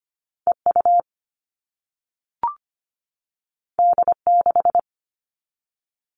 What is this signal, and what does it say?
Morse code, heard as a keyed tone, about 25 wpm, 700 Hz. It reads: EU D6
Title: All Cut Numbers - 2-Digit 25wpm